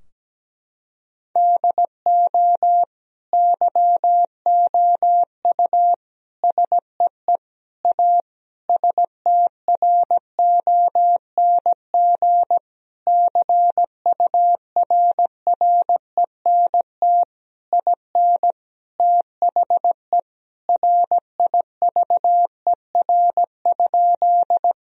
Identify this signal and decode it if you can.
Morse code, heard as a keyed tone, about 17 wpm, 700 Hz. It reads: DO YOU SEE A STRONG CURRENT IN THE RIVER?
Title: Sentences from Top 1000 Words - Review Part 4 - 17wpm